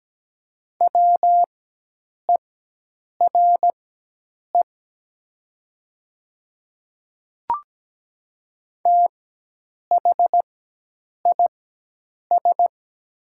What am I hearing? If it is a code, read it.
Morse code, heard as a keyed tone, about 17 wpm, spaced as for 8 wpm, 700 Hz. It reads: WERE THIS
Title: 4x Character Spacing - Top 100 Words - 17wpm